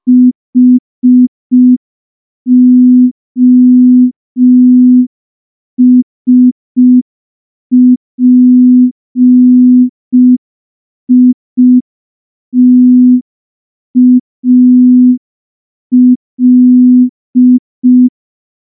Morse code, heard as a keyed tone, about 5 wpm, 250 Hz. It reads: HOSPITAL
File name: morse / morse-training-05WPM